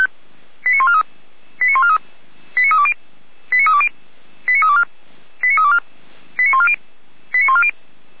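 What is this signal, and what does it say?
Five-tone selective call (ZVEI), heard as quick runs of five tones, tones of about 70 ms. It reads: 89143 89143 89320 89320 89324 89324 89140 89140
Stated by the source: ZVEI five-tone call 89514 received by SDR, 455 channel.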